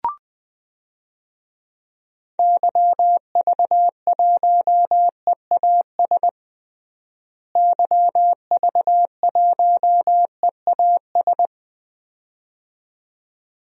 Morse code, heard as a keyed tone, about 20 wpm, 700 Hz. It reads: YV1EAS YV1EAS
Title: DXCC-100 Call Signs - 6 Characters - Repeated 2x - 20wpm